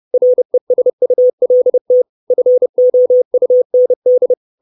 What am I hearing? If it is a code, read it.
Morse code, heard as a keyed tone, about 30 wpm, 500 Hz. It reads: RESULT FOUND
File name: 632 Words 30WPM